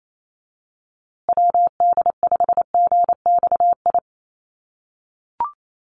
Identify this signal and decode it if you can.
Morse code, heard as a keyed tone, about 28 wpm, 700 Hz. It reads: WD5GXI
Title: USA Call Signs Repeated 2x 28wpm